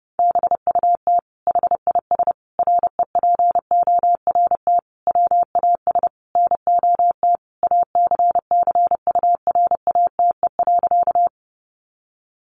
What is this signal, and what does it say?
Morse code, heard as a keyed tone, about 30 wpm, 700 Hz. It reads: BUT HIS REPORT WAS NOT ACCURATE.
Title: Sentences from Top 1200 Words 30wpm